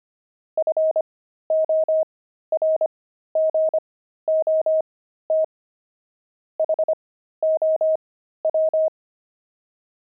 Morse code, heard as a keyed tone, about 25 wpm, spaced as for 13 wpm, 630 Hz. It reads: FORGOT HOW
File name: pangram-fs-13-cs-25-devils